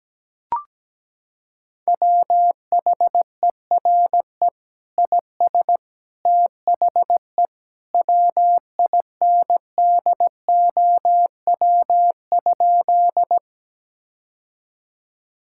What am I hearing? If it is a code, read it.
Morse code, heard as a keyed tone, about 17 wpm, 700 Hz. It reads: WHERE IS THE WINDOW?
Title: Sentences from Top 1600 Words 17wpm